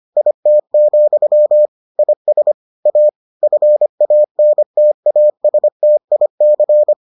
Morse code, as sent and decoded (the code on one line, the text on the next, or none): .. - --..-- / .. ... / .- / ..-. .- -. - .- ... - .. -.-.
IT, IS A FANTASTIC